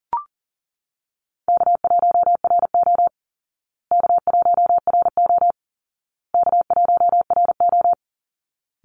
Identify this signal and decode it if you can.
Morse code, heard as a keyed tone, about 37 wpm, 700 Hz. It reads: K1RO K1RO K1RO
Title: Easy Call Signs Repeated 3x - 40wpm